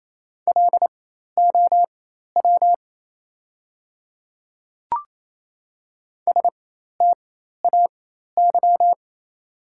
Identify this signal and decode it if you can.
Morse code, heard as a keyed tone, about 28 wpm, spaced as for 13 wpm, 690 Hz. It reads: LOW STAY
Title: 4x Character Spacing - Top 500 Words - 28wpm